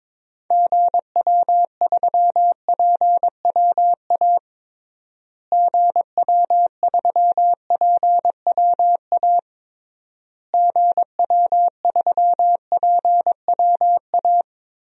Morse code, heard as a keyed tone, about 22 wpm, 700 Hz. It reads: GW3PWA GW3PWA GW3PWA
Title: DXCC-100 Call Signs - 6 Characters - Repeated 3x - 22wpm